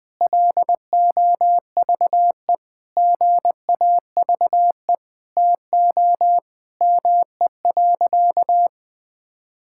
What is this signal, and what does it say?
Morse code, heard as a keyed tone, about 20 wpm, 700 Hz. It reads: LOVE GAVE TO ME.
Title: Sentences with 8-Word Phrases - 20wpm